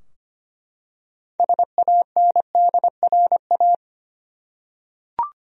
Morse code, heard as a keyed tone, about 25 wpm, 700 Hz. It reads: SANDRA
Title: USA Common Names - 25wpm